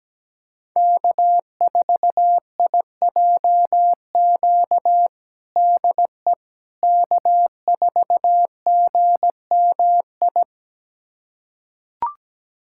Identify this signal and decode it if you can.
Morse code, heard as a keyed tone, about 17 wpm, 700 Hz. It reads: K4IJQ DE K4GMI
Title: USA Callsign DE Callsign 17wpm